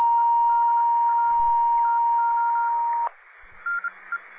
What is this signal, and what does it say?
Morse code, heard as a keyed tone, about 26 wpm, spaced as for 20 wpm, 1400 Hz, with noise. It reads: EDA TO NE